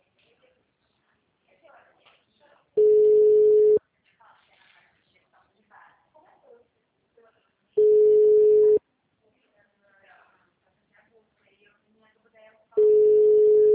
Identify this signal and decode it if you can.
Ringback tone (Europe, 425 Hz), with speech.